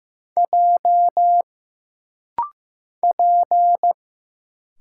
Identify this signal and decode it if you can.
Morse code, heard as a keyed tone, about 15 wpm, 700 Hz. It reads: J P
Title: Single Letters - Rapid-Fire 15wpm